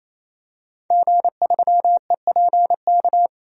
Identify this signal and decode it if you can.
Morse code, heard as a keyed tone, about 27 wpm, 700 Hz. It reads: G3EPK
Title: DXCC-100 Call Signs - 5 Characters - 28wpm